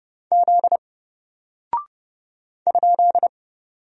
Morse code, heard as a keyed tone, about 30 wpm, 700 Hz. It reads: Z ?
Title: Single Character (Letter:Number:Punctuation:Prosign) 30wpm